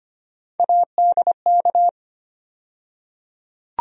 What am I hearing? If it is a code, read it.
Morse code, heard as a keyed tone, about 25 wpm, 700 Hz. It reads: ADK
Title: USA Call Sign Postfixes - 3 Letters 25wpm